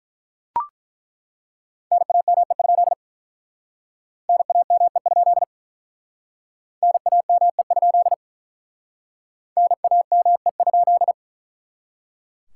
Morse code, heard as a keyed tone, about 48 wpm, 700 Hz. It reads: NAME? NAME? NAME? NAME?